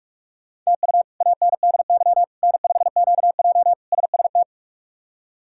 Morse code, heard as a keyed tone, about 42 wpm, 700 Hz. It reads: TU ANDY N5QJ SST
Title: K1USN SST QSO Practice - 45wpm